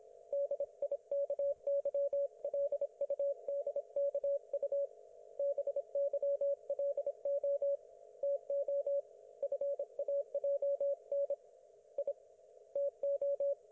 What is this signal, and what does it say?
Morse code, heard as a keyed tone, about 26 wpm, 560 Hz. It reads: DIKY LUDKU BYLO TO FAJN I TO
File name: Archiv / CUC194 / ok1fhi cul194